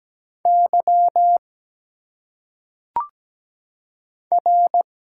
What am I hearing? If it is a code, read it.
Morse code, heard as a keyed tone, about 17 wpm, 700 Hz. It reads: Y R
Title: Single Letters - Phonetic Pronunciation 17wpm